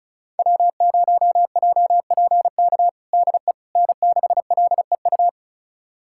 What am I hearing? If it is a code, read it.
Morse code, heard as a keyed tone, about 34 wpm, 700 Hz. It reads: W0JPK DE N6LEU
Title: USA Callsign DE Callsign - 5 Characters - 35wpm